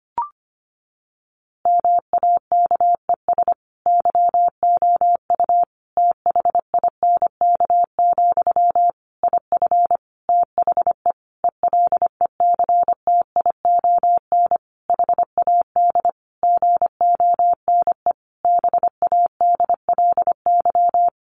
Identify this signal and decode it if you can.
Morse code, heard as a keyed tone, about 25 wpm, 700 Hz. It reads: MAKES YOU THINK, IF THE ELECTION HAD GONE BADLY
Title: A Sound Of Thunder 25wpm